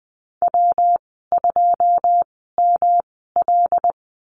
Morse code, heard as a keyed tone, about 20 wpm, spaced as for 15 wpm, 700 Hz. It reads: W2ML
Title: 147 Lesson 24g - Letter y - Call Signs 20wpm